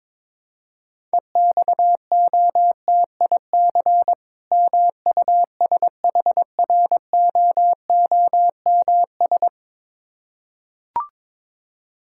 Morse code, heard as a keyed tone, about 22 wpm, 690 Hz. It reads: EXOTIC MUSHROOMS
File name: Difficult Words with Hints - 22wpm